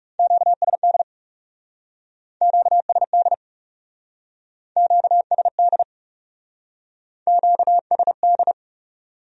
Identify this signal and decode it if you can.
Morse code, heard as a keyed tone, about 37 wpm, 700 Hz. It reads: QSD QSD QSD QSD
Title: Speed-Racing Q-Codes - 30wpm